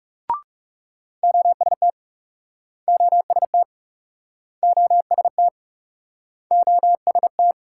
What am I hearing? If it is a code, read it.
Morse code, heard as a keyed tone, about 41 wpm, 700 Hz. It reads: OST OST OST OST